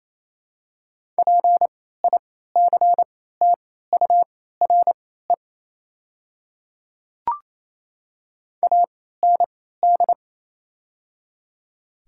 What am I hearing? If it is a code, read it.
Morse code, heard as a keyed tone, about 28 wpm, spaced as for 16 wpm, 700 Hz. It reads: PICTURE AND
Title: 3x Character Spacing - Top 300 Words - 28wpm